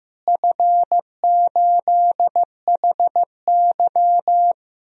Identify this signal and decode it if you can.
Morse code, heard as a keyed tone, about 15 wpm, 700 Hz. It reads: F8HY
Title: DXCC-100 Call Signs - 4 Characters - Repeated 3x - 15wpm